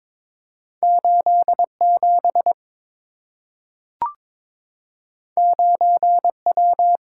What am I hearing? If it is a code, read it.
Morse code, heard as a keyed tone, about 22 wpm, 700 Hz. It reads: 87 9W
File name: practice / 22wpm / Two Letter-Number Combinations - 22wpm